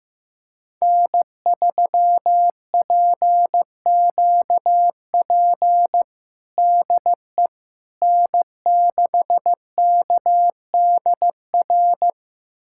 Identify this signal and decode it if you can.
Morse code, heard as a keyed tone, about 15 wpm, 700 Hz. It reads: N3PQP DE N6KDR
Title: USA Callsign DE Callsign - 5 Characters - 15wpm